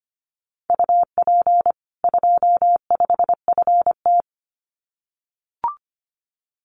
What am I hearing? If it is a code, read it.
Morse code, heard as a keyed tone, about 25 wpm, 700 Hz. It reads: UP 25 FT
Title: Common QSO Phrases 25wpm